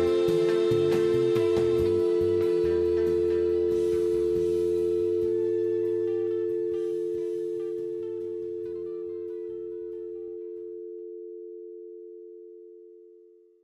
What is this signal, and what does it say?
Dial tone (North America), 350 + 440 Hz.